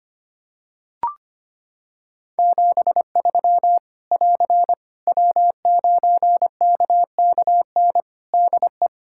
Morse code, heard as a keyed tone, about 25 wpm, 700 Hz. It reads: 73 <AR> W9KKN DE
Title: K1USN SST QSO Practice 25wpm